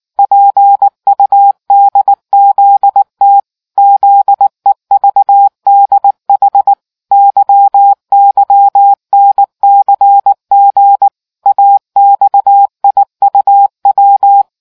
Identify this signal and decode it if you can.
Morse code, heard as a keyed tone, about 19 wpm, 800 Hz. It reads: PUDZT ZEVDH YYNCG AXIUW